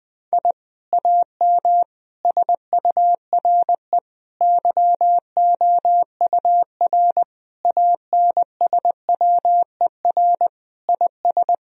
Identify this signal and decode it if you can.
Morse code, heard as a keyed tone, about 20 wpm, 700 Hz. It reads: I AM SURE YOUR ANSWER IS
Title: Sentences from Top 1000 Words - Review Part 2 - 20wpm